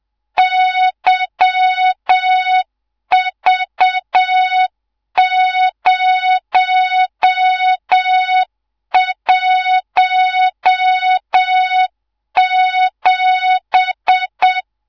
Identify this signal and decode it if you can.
Morse code, heard as a keyed tone, about 7 wpm, 3700 Hz, with noise. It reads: YV017
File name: Morze-HA2MN / hangp / 2 03 GYAKORLAT